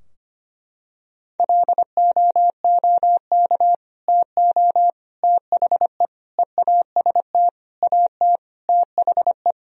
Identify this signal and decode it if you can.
Morse code, heard as a keyed tone, about 25 wpm, 700 Hz. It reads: LOOK TO THE EAST AT THE
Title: Sentences from Top 1000 Words - Review Part 3 - 25wpm